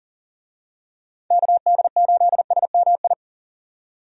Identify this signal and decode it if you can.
Morse code, heard as a keyed tone, about 40 wpm, 700 Hz. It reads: KD8SMI
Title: USA Call Signs - 6 Characters - 40wpm